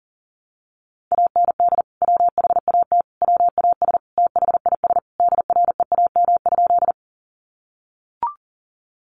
Morse code, heard as a keyed tone, about 40 wpm, 700 Hz. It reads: AND WHAT WAS THIS DREAM?
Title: Sentences from Top 1500 Words 40wpm